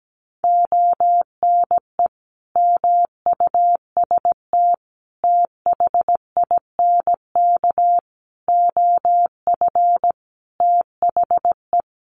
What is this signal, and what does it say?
Morse code, heard as a keyed tone, about 17 wpm, 700 Hz. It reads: ONE MUST THINK OF THE
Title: Sentences from Top 700 Words 17wpm